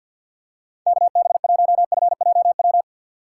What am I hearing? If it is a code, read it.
Morse code, heard as a keyed tone, about 51 wpm, 700 Hz. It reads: KB1FJW
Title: USA Call Signs - 3x Character Spacing - 50wpm